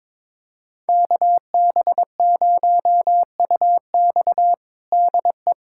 Morse code, heard as a keyed tone, about 22 wpm, 700 Hz. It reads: KB0UX DE
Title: USA Callsign DE Callsign 22wpm